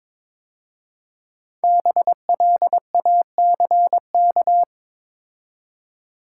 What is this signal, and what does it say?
Morse code, heard as a keyed tone, about 22 wpm, 690 Hz. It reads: BLACK
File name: 5 Letter Words - First 3 Letters Given - 22wpm